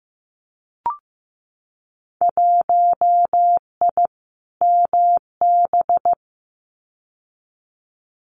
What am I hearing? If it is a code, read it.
Morse code, heard as a keyed tone, about 15 wpm, 700 Hz. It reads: 1I MB
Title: Winter Field Day Exchanges Encore 15wpm